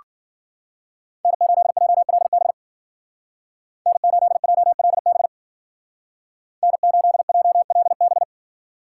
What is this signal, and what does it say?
Morse code, heard as a keyed tone, about 49 wpm, 700 Hz. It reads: N8JLB N8JLB N8JLB
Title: Speed Racing - USA 1x3 Call Signs 40wpm